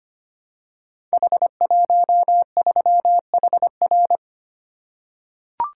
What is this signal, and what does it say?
Morse code, heard as a keyed tone, about 25 wpm, 700 Hz. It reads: H13HR
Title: Code Groups - 5 Letter:Numbers - 25wpm